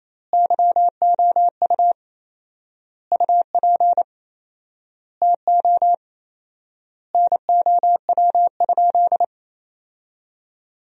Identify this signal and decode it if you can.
Morse code, heard as a keyed tone, about 28 wpm, 700 Hz. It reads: YOU UP TO NOW?
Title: 4x Word-Spacing - Sentences from Top 100 Words - 28wpm